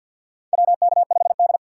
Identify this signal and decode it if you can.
Morse code, heard as a keyed tone, about 54 wpm, 700 Hz. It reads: WK5D
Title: USA Call Signs - 4 Characters - 6x Character Spacing - 50wpm